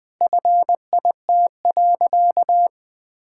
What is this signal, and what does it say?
Morse code, heard as a keyed tone, about 21 wpm, 700 Hz. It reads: FIT.
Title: Sentences with 2-Word Phrases - 20wpm